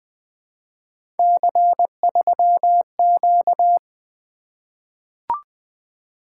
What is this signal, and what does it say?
Morse code, heard as a keyed tone, about 20 wpm, 700 Hz. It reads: C3Q
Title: Three Letter-Number Combinations - 20wpm